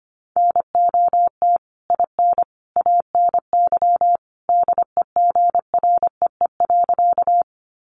Morse code, heard as a keyed tone, about 25 wpm, 700 Hz. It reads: NOT IN ANY DEGREE.
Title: Sentences from Top 1500 Words 25wpm